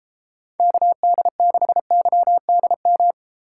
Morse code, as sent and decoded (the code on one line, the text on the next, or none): -.- -.. -.... -.-- -.. --
KD6YDM